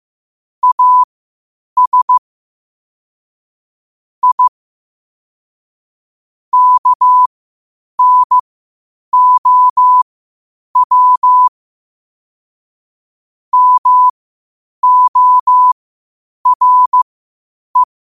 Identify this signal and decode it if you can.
Morse code, heard as a keyed tone, about 15 wpm, spaced as for 8 wpm, 1000 Hz. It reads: AS I KNOW MORE
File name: qotdfiles / 8wpm